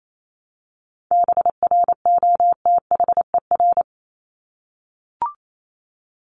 Morse code, heard as a keyed tone, about 28 wpm, 700 Hz. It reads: BROTHER
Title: Top 1751-2000 Words 28wpm